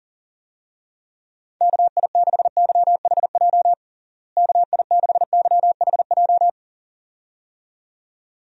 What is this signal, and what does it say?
Morse code, heard as a keyed tone, about 40 wpm, 700 Hz. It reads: KI6YHJ KI6YHJ
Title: USA Call Signs - 6 Characters - Repeated 2x - 40wpm